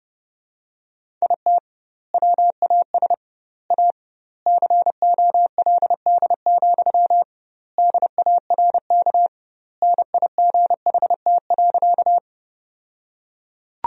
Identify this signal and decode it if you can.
Morse code, heard as a keyed tone, about 32 wpm, 700 Hz. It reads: IT WAS A COLD, DARK NIGHT.
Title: 2x Word-Spacing - Sentences from Top 500 Words - 30wpm